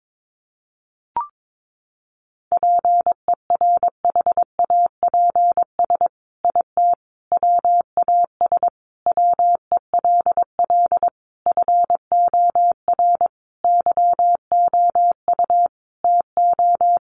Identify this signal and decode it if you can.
Morse code, heard as a keyed tone, about 22 wpm, 700 Hz. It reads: PERHAPS IT WAS WELL FOR YOU TO
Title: Sentences from Top 1400 Words 22wpm